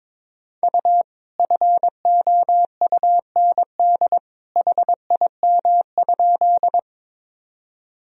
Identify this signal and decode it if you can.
Morse code, heard as a keyed tone, about 22 wpm, 700 Hz. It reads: U FOUND HIM?
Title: Sentences from Top 300 Words and Abbreviations 22wpm